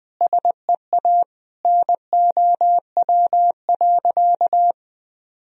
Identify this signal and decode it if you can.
Morse code, heard as a keyed tone, about 20 wpm, 700 Hz. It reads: SEA NOW.
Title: Sentences from Top 1000 Words - Review Part 1 - 20wpm